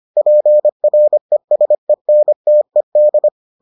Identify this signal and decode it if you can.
Morse code, heard as a keyed tone, about 25 wpm, 600 Hz. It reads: PRESENTED